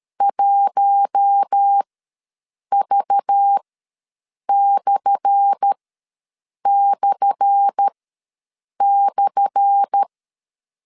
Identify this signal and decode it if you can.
Morse code, heard as a keyed tone, about 13 wpm, spaced as for 7 wpm, 800 Hz. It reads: 1V///
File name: LEZIONI / 10 SIGNS / 10A SLASH ws 12wpm 3443